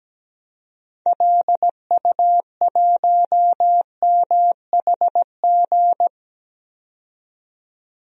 Morse code, heard as a keyed tone, about 17 wpm, 700 Hz. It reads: LU1MHG
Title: DXCC-100 Call Signs - 6 Characters - 17wpm